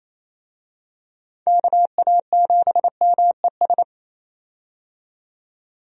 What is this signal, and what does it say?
Morse code, heard as a keyed tone, about 28 wpm, 700 Hz. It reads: KA7MES
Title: USA Call Signs - 6 Characters - 28wpm